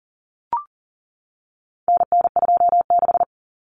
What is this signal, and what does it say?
Morse code, heard as a keyed tone, about 40 wpm, 700 Hz. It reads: NN26
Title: Common Cut Numbers - 4-Digit 40wpm